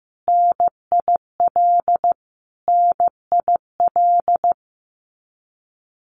Morse code, heard as a keyed tone, about 15 wpm, 700 Hz. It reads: NIL NIL
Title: Abbreviations Repeated 3x - 15wpm